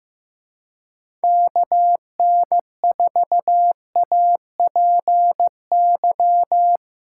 Morse code, heard as a keyed tone, about 15 wpm, 700 Hz. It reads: KN4APY
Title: USA Call Signs - 6 Characters - 3x Character Spacing - 15wpm